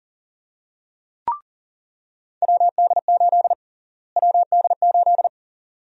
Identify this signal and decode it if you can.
Morse code, heard as a keyed tone, about 40 wpm, 700 Hz. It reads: WD8 WD8
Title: USA Call Sign Prefixes - Repeated 2x - 40wpm